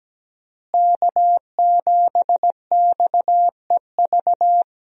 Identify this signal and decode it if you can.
Morse code, heard as a keyed tone, about 17 wpm, 700 Hz. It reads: K7XEV
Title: USA Call Signs - 5 Characters - Repeated 2x - 17wpm